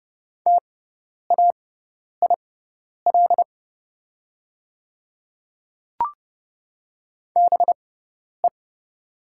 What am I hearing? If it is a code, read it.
Morse code, heard as a keyed tone, about 30 wpm, spaced as for 10 wpm, 690 Hz. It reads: TAIL BE
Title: 6x Character Spacing - Top 500 Words - 30wpm